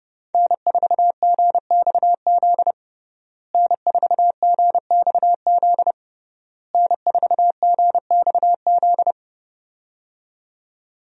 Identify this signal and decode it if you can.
Morse code, heard as a keyed tone, about 30 wpm, 700 Hz. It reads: N4GXZ N4GXZ N4GXZ
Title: USA Call Signs - 5 Characters - Repeated 3x - 30wpm